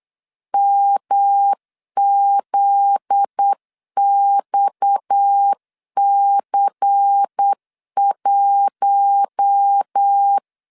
Morse code, heard as a keyed tone, about 8 wpm, 800 Hz. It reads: MZXC1
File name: LEZIONI / 7 GF9Z / 7 GF9Z 8wpm 3213